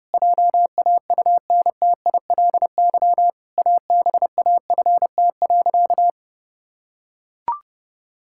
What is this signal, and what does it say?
Morse code, heard as a keyed tone, about 30 wpm, 700 Hz. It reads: JAUNTILY ABAFT.
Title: Pangram Challenge v2 - 30wpm